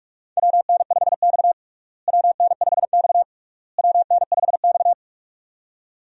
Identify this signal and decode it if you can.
Morse code, heard as a keyed tone, about 46 wpm, 700 Hz. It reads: WN5X WN5X WN5X
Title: USA Call Signs - 4 Characters - Repeated 3x - 45wpm